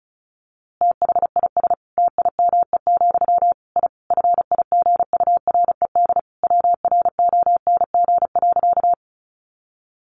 Morse code, heard as a keyed tone, about 34 wpm, 700 Hz. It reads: THIS TIME, I FIGURED WRONG.
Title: A Sound Of Thunder 35wpm